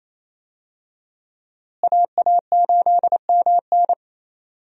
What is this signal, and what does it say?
Morse code, heard as a keyed tone, about 28 wpm, 700 Hz. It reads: AA8MN
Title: USA Call Signs - 5 Characters - Repeated 3x - 28wpm